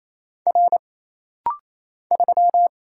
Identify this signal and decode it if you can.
Morse code, heard as a keyed tone, about 27 wpm, 700 Hz. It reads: R 3
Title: Single Letter-Number Alt - Rapid-Fire 28wpm